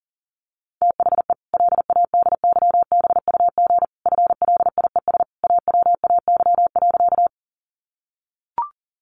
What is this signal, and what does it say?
Morse code, heard as a keyed tone, about 40 wpm, 700 Hz. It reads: THE LADYBUG FLIES AWAY.
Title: Additional Easy Phrases 40wpm